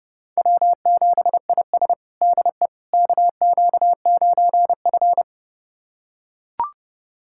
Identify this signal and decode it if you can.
Morse code, heard as a keyed tone, about 30 wpm, 700 Hz. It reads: W7IS DE KQ9F
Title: USA Callsign DE Callsign - 4 Characters - 30wpm